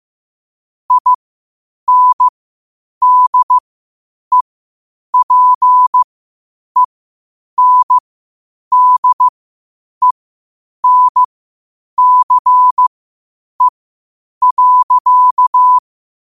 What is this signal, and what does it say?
Morse code, heard as a keyed tone, about 15 wpm, spaced as for 8 wpm, 1000 Hz. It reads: INDEPENDENCE.
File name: qotdfiles / 8wpm